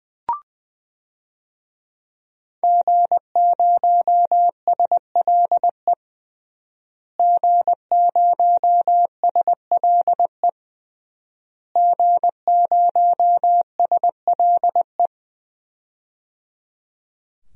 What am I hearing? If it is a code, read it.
Morse code, heard as a keyed tone, about 20 wpm, 700 Hz. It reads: G0SLE G0SLE G0SLE
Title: DXCC-100 Call Signs - 5 Characters - Repeated 3x - 20wpm